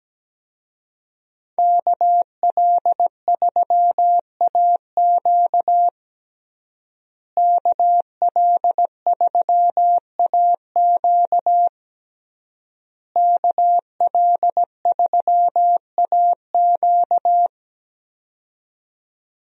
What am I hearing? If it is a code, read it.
Morse code, heard as a keyed tone, about 17 wpm, 700 Hz. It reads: KL3AQ KL3AQ KL3AQ
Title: USA Call Signs - 5 Characters - Repeated 3x - 17wpm